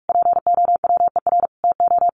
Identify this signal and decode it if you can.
Morse code, heard as a keyed tone, about 44 wpm, 700 Hz. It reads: POWER TO
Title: Sentences from Top 1200 Words 45wpm